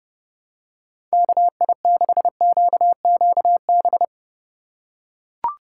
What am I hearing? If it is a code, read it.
Morse code, heard as a keyed tone, about 31 wpm, 700 Hz. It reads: KI6QQB
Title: USA Call Signs - 6 Characters - 4x Character Spacing - 30wpm